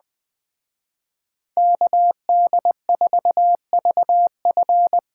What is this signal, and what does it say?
Morse code, heard as a keyed tone, about 20 wpm, 700 Hz. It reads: KD4VF